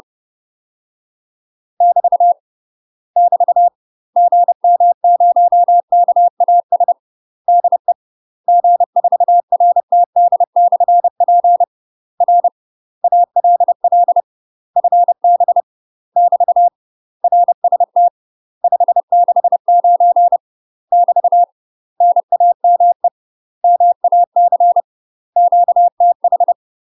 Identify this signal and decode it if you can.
Morse code, heard as a keyed tone, about 30 wpm, 700 Hz. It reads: <BT> <BT> GM0KAS DE G4RTD/P R ALL FB <BT> RST 569 <BT> NAME MAC QTH
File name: UK MASTER QSOs TRAINING / 30WPM / UK MASTER QSOs 0050